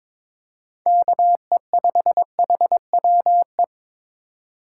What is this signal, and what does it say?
Morse code, heard as a keyed tone, about 22 wpm, 700 Hz. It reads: KE5HWE
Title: USA Call Signs - 6 Characters - 3x Character Spacing - 22wpm